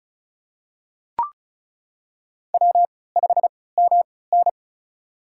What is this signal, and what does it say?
Morse code, heard as a keyed tone, about 36 wpm, spaced as for 20 wpm, 700 Hz. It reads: W5MN
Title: USA Call Signs - 3x Character Spacing - 35wpm